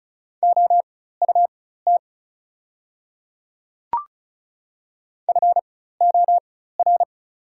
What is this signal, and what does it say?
Morse code, heard as a keyed tone, about 34 wpm, spaced as for 16 wpm, 700 Hz. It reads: OUT FOR
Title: 4x Character Spacing - Top 300 Words - 35wpm